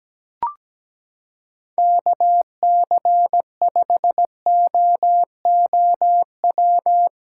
Morse code, heard as a keyed tone, about 17 wpm, 700 Hz. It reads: KC5OOW
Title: USA Callsign DE Callsign - 6 Characters - 17wpm